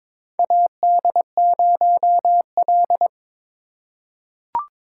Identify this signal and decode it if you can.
Morse code, heard as a keyed tone, about 22 wpm, 700 Hz. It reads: AD0L
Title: USA Call Signs - 4 Characters - 3x Character Spacing - 22wpm